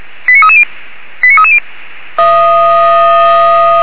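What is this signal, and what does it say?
Five-tone selective call (ZVEI), heard as quick runs of five tones, tones of about 70 ms. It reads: 89330 89330
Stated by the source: ZVEI five-tone call received on SDR.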